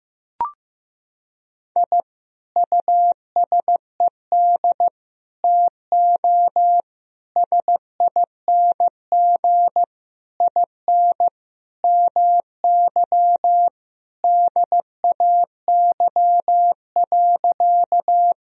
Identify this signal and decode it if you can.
Morse code, heard as a keyed tone, about 15 wpm, 700 Hz. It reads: I USED TO SING IN MY DAY.
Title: Sentences from Top 1000 Words - Review Part 2 - 15wpm